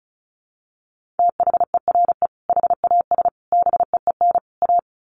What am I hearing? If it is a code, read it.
Morse code, heard as a keyed tone, about 34 wpm, 700 Hz. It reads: THERE HAS BEEN A
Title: Sets of 4 Words 35wpm